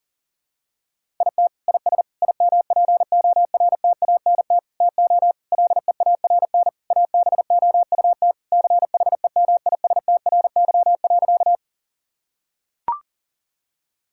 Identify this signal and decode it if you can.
Morse code, heard as a keyed tone, about 40 wpm, 700 Hz. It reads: IT IS IMPORTANT TO LEARN ABOUT CHEMISTRY.
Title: Sentences from Top 1000 Words - Review Part 6 - 40wpm